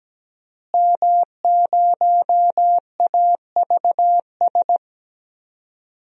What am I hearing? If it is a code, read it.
Morse code, heard as a keyed tone, about 17 wpm, 700 Hz. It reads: M0AVS
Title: DXCC-100 Call Signs - 5 Characters - Repeated 3x - 17wpm